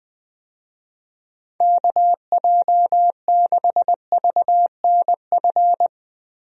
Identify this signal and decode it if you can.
Morse code, heard as a keyed tone, about 20 wpm, 700 Hz. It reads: KJ6VNF
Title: USA Call Signs - 6 Characters - Repeated 3x - 20wpm